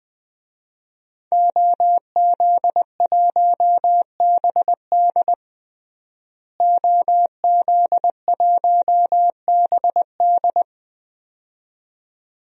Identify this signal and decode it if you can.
Morse code, heard as a keyed tone, about 20 wpm, 700 Hz. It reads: OZ1BD OZ1BD
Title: DXCC-100 Call Signs - 5 Characters - Repeated 2x - 20wpm